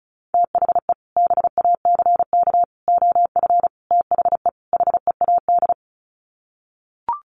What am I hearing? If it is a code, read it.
Morse code, heard as a keyed tone, about 34 wpm, 700 Hz. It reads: THE BACK OF THE HEAD
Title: Sets of 5 Words Encore2 35wpm